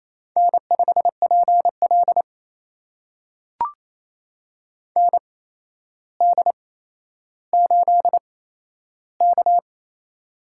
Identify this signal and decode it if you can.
Morse code, heard as a keyed tone, about 28 wpm, spaced as for 12 wpm, 700 Hz. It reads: N5PL ND8K
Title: USA Call Signs - 4 Characters - 8x Character Spacing - 28wpm